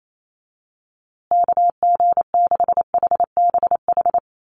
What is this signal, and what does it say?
Morse code, heard as a keyed tone, about 28 wpm, 700 Hz. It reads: KG6HBH
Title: USA Call Signs Repeated 2x 28wpm